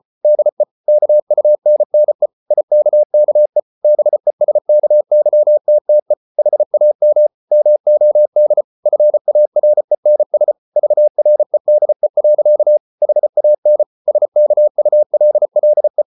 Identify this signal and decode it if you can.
Morse code, heard as a keyed tone, about 34 wpm, 600 Hz. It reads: DE KUNNE IKKE BESKYTTE HAM MOD FARENS VREDE. HAN SKULLE
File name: Ferie QRQ 34wpm